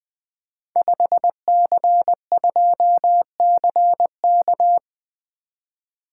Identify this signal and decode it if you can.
Morse code, heard as a keyed tone, about 20 wpm, 700 Hz. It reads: 5C2CK
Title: Code Groups - 5 Letter:Numbers - 20wpm